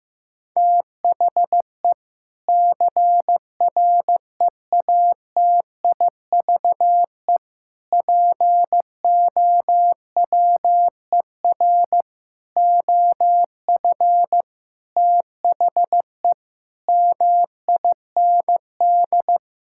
Morse code, heard as a keyed tone, about 15 wpm, 700 Hz. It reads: THE CREATIVE POWER OF THE MIND
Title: Sentences from Top 1000 Words - Review Part 5 - 15wpm